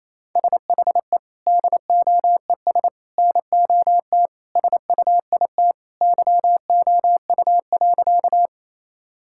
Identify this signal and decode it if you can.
Morse code, heard as a keyed tone, about 28 wpm, 700 Hz. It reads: SHE DOES NOT SUIT YOU.
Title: Sentences from Top 1600 Words 28wpm